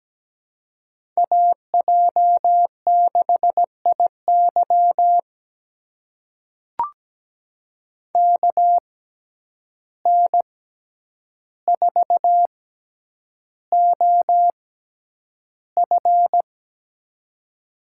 Morse code, heard as a keyed tone, about 17 wpm, spaced as for 9 wpm, 700 Hz. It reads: AJ6IY KN4OF
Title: USA Call Signs - 5 Characters - 6x Character Spacing - 17wpm